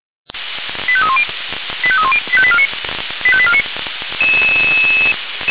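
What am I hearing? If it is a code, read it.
Five-tone selective call (ZVEI), heard as quick runs of five tones, tones of about 70 ms. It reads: 85310 85310 85849 85849
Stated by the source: ZVEI five-tone selective call 85849, received by SDR.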